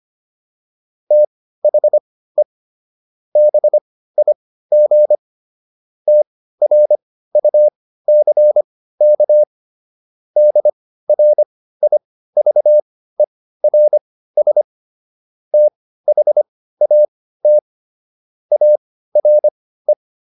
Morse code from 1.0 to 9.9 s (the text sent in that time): THE BIG TRUCK